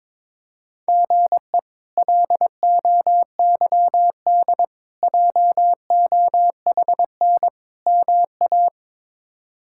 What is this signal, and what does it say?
Morse code, heard as a keyed tone, about 22 wpm, 700 Hz. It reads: GE LOYD JOHN MA
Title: K1USN SST QSO Practice - 22wpm